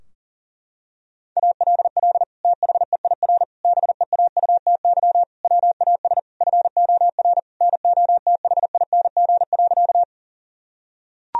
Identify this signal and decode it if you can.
Morse code, heard as a keyed tone, about 37 wpm, 700 Hz. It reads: ALL THEIR BEAUTY WAS FOR NOTHING.
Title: Sentences from Top 1000 Words - Review Part 2 - 40wpm